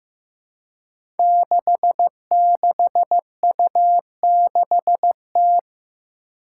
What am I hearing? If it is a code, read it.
Morse code, heard as a keyed tone, about 15 wpm, 700 Hz. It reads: 66U6T
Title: Five Challenging Letter-Number Combinations 15wpm